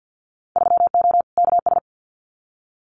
Morse code, heard as a keyed tone, about 50 wpm, 700 Hz. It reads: 3O KS
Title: Winter Field Day Exchanges Encore 50wpm